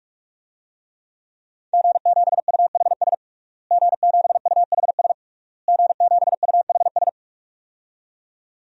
Morse code, heard as a keyed tone, about 41 wpm, 700 Hz. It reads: G7UHS G7UHS G7UHS
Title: DXCC-100 Call Signs - 5 Characters - Repeated 3x - 45wpm